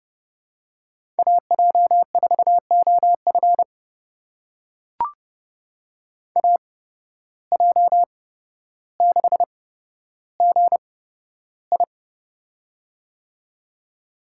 Morse code, heard as a keyed tone, about 31 wpm, spaced as for 13 wpm, 700 Hz. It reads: AJ4OF AJ6GI
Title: USA Call Signs - 5 Characters - 8x Character Spacing - 30wpm